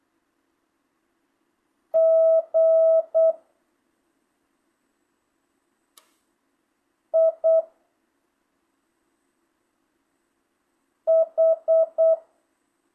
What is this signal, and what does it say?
Morse code, heard as a keyed tone, about 8 wpm, 660 Hz. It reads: G I H
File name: jh8cbh / 2025cw